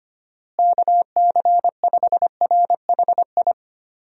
Morse code, heard as a keyed tone, about 25 wpm, 700 Hz. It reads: KC5RHI
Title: USA Call Signs - 6 Characters - Repeated 3x - 25wpm